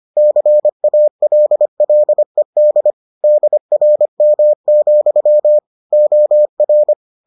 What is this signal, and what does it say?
Morse code, heard as a keyed tone, about 25 wpm, 600 Hz. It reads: CALLED DRM, OR